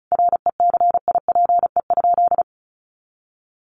Morse code, heard as a keyed tone, about 34 wpm, 700 Hz. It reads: RECIPE?
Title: Sentences from Top 1200 Words 35wpm